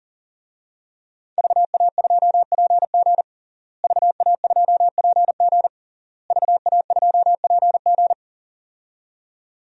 Morse code, heard as a keyed tone, about 40 wpm, 700 Hz. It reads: VA2PG VA2PG VA2PG
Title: DXCC-100 Call Signs - 5 Characters - Repeated 3x - 40wpm